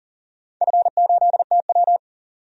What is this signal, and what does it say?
Morse code, heard as a keyed tone, about 38 wpm, 700 Hz. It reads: F8TW
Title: DXCC-100 Call Signs - 4 Characters - 40wpm